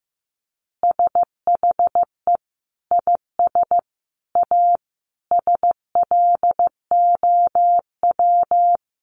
Morse code, heard as a keyed tone, about 15 wpm, 700 Hz. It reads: SHE IS A SLOW
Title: Additional Easy Phrases 15wpm